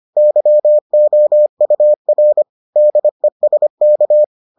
Morse code, heard as a keyed tone, about 25 wpm, 600 Hz. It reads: YOUR DESK